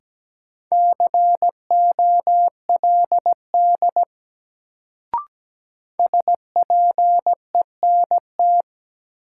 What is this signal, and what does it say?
Morse code, heard as a keyed tone, about 17 wpm, 700 Hz. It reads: COLD SPENT
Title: Sets of 5 Words - New Word Drill - 17wpm